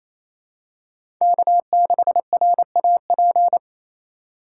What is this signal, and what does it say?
Morse code, heard as a keyed tone, about 28 wpm, 700 Hz. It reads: K6RAP
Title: USA Call Signs - 5 Characters - 2x Character Spacing - 28wpm